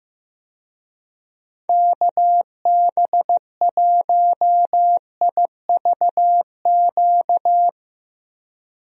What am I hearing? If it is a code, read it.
Morse code, heard as a keyed tone, about 15 wpm, 700 Hz. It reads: KB1IVQ
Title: USA Call Signs - 6 Characters - Repeated 3x - 15wpm